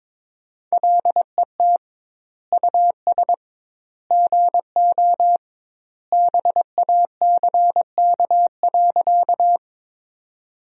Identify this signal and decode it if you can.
Morse code, heard as a keyed tone, about 21 wpm, 700 Hz. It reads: LET US GO BACK.
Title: 2x Word-Spacing - Sentences from Top 300 Words - 22wpm